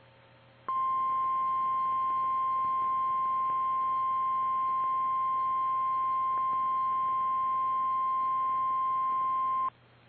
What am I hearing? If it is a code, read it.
Emergency Alert System (SAME) tones (the 1050 Hz attention tone). It is heard from 0.6 to 9.7 s.
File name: eas / KKPRKXPN / KKPRKXPN